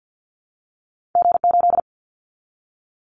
Morse code, heard as a keyed tone, about 50 wpm, 700 Hz. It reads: G8
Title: Callsign Prefixes for Top 25 DXCC - Repeated 3x - 50wpm